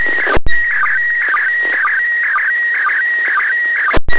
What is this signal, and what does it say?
SSTV (slow-scan television), PD 120.
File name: posts / receiving-sstv-images-from-iss / iss sstv2